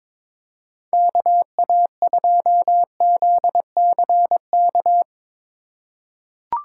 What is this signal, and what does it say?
Morse code, heard as a keyed tone, about 22 wpm, 700 Hz. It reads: KA2ZCK
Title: USA Call Signs - 3x Character Spacing - 22wpm